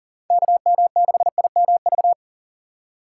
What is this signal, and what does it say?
Morse code, heard as a keyed tone, about 40 wpm, 700 Hz. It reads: KM6IMV
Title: USA Call Signs - 6 Characters - 3x Character Spacing - 40wpm